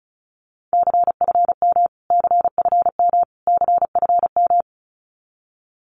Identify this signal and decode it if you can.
Morse code, heard as a keyed tone, about 34 wpm, 700 Hz. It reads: CFM CFM CFM
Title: Abbreviations Repeated 3x - 35wpm